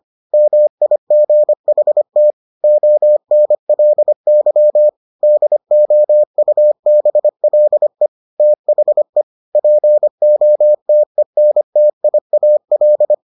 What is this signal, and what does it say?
Morse code, heard as a keyed tone, about 25 wpm, 600 Hz. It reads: MIGHT ONLY DOUBLE THE POTENTIAL